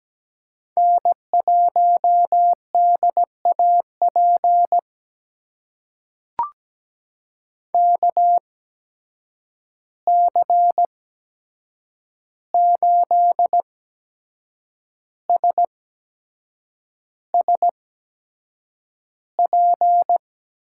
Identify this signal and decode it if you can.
Morse code, heard as a keyed tone, about 17 wpm, spaced as for 7 wpm, 700 Hz. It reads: N1DAP KC8SSP
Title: USA Call Signs - 8x Character Spacing - 17wpm